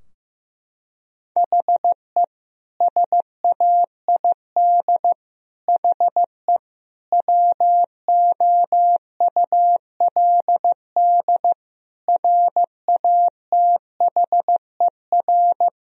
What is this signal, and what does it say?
Morse code, heard as a keyed tone, about 15 wpm, 700 Hz. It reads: HE SAID HE WOULD RATHER
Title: Sentences from Top 1000 Words - Review Part 4 - 15wpm